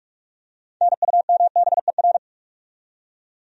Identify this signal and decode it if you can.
Morse code, heard as a keyed tone, about 48 wpm, 700 Hz. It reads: NUMBER
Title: Top 100 Words Repeated 45wpm